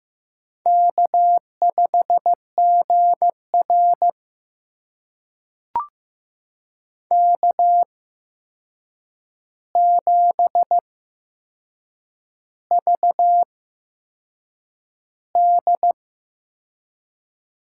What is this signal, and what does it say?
Morse code, heard as a keyed tone, about 15 wpm, spaced as for 6 wpm, 700 Hz. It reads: K5GR K7VD
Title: USA Call Signs - 4 Characters - 8x Character Spacing - 15wpm